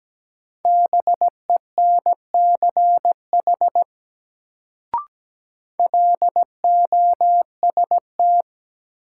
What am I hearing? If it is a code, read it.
Morse code, heard as a keyed tone, about 17 wpm, 700 Hz. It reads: BENCH LOST
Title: Sentences from Top 1400 Words - New Word Drill - 17wpm